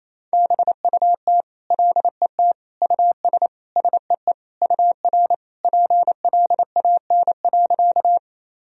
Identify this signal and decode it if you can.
Morse code, heard as a keyed tone, about 28 wpm, 700 Hz. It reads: BUT LET US SEE UR PLAN.
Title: Sentences from Top 500 Words and Abbreviations 28wpm